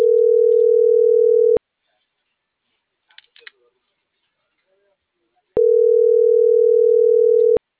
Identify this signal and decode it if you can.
Ringback tone (North America), 440 + 480 Hz.
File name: CENTRAL NMP / out melanie.diaznmp - .37205